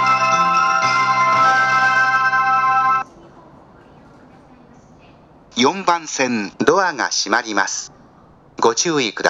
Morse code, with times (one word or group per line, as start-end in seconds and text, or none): none